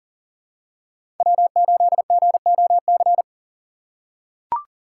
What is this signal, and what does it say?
Morse code, heard as a keyed tone, about 40 wpm, 700 Hz. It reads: W8GOC
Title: USA Call Signs - 5 Characters - 6x Character Spacing - 40wpm